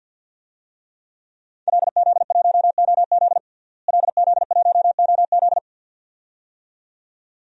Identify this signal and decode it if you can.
Morse code, heard as a keyed tone, about 49 wpm, 700 Hz. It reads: LZ1OZ LZ1OZ
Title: DXCC-100 Call Signs - 5 Characters - Repeated 2x - 50wpm